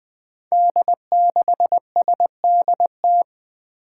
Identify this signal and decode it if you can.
Morse code, heard as a keyed tone, about 21 wpm, 700 Hz. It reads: D6SDT
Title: Five Challenging Letter-Number Combinations 20wpm